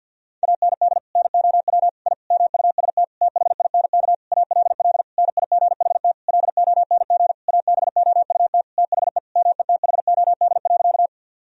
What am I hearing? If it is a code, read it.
Morse code, heard as a keyed tone, about 52 wpm, 700 Hz. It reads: AND NOW I MUST THINK ALL NIGHT LONG ABOUT THE METHOD.
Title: Sentences from Top 1000 Words - Review Part 3 - 50wpm